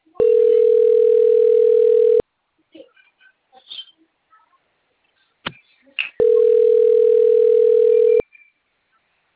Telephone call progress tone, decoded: ringback tone

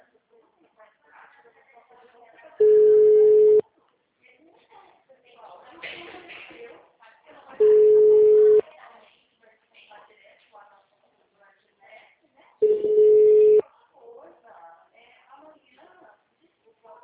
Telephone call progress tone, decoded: ringback tone

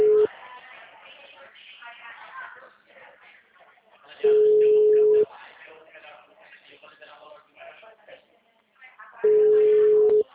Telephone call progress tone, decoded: ringback tone

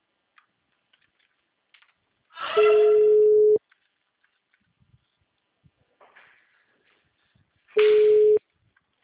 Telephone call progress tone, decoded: ringback tone

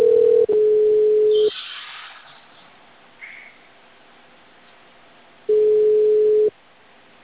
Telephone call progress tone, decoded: ringback tone

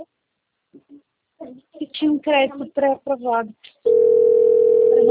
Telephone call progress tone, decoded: ringback tone

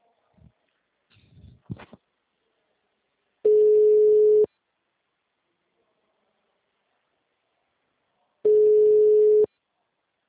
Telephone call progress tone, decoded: ringback tone

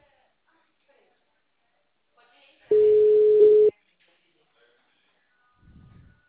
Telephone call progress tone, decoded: ringback tone

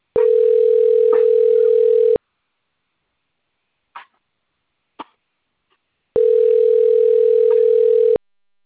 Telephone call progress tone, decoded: ringback tone

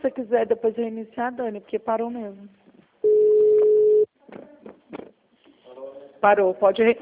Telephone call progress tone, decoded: ringback tone

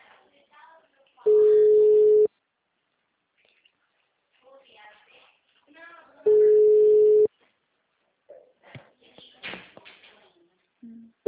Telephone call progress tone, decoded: ringback tone

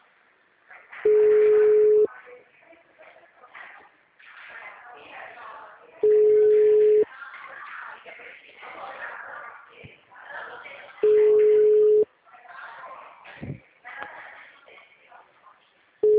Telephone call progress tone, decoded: ringback tone

